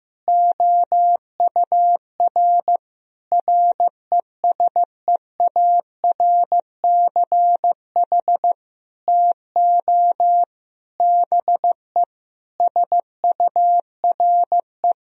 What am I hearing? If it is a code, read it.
Morse code, heard as a keyed tone, about 16 wpm, 700 Hz. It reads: OUR RESEARCH TO BE SURE
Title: Sentences from Top 1000 Words - Review Part 4 - 15wpm